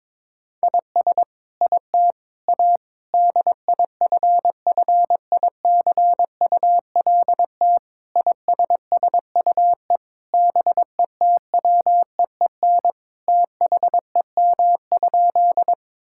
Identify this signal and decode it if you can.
Morse code, heard as a keyed tone, about 22 wpm, 700 Hz. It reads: IS IT A DIFFICULT ISSUE BETWEEN THEM?
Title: Sentences from Top 1000 Words - Review Part 6 - 22wpm